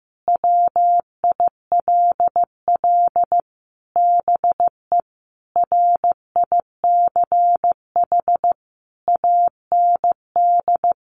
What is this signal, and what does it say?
Morse code, heard as a keyed tone, about 14 wpm, 700 Hz. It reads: WILL BE RICH AND
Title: Sentences from Top 1100 Words 15wpm